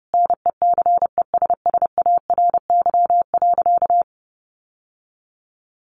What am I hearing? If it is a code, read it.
Morse code, heard as a keyed tone, about 30 wpm, 700 Hz. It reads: NECESSARY.
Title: Sentences from Top 800 Words 30wpm